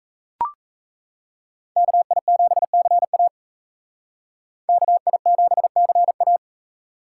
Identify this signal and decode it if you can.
Morse code, heard as a keyed tone, about 35 wpm, 700 Hz. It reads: KI7CA KI7CA
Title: Speed Racing - USA Call Signs 28wpm